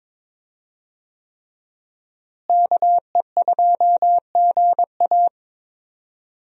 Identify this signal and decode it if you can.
Morse code, heard as a keyed tone, about 22 wpm, 700 Hz. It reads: KE2GA
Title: USA Call Signs - 5 Characters - Repeated 3x - 22wpm